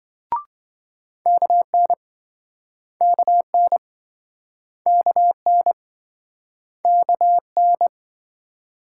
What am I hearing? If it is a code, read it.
Morse code, heard as a keyed tone, about 27 wpm, 700 Hz. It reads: KN KN KN KN